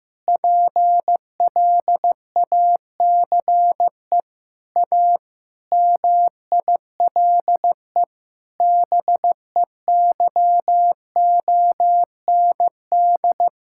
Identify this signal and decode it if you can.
Morse code, heard as a keyed tone, about 15 wpm, 700 Hz. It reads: PLACE A MILE BEYOND
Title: Sentences from Top 1000 Words - Review Part 6 - 15wpm